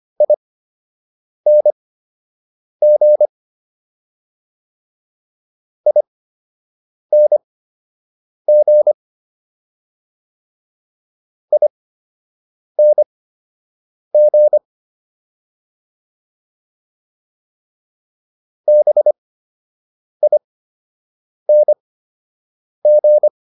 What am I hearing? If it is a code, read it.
Morse code, heard as a keyed tone, about 25 wpm, spaced as for 7 wpm, 600 Hz. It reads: ING ING ING BING